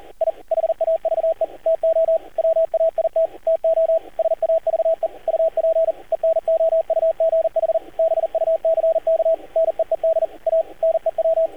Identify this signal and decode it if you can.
Morse code, heard as a keyed tone, about 38 wpm, 650 Hz, with radio noise. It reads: I HAVE TO WAIT TO SAVE UP ENOUGH BUCK NEED A NEW
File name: archived / CRSnet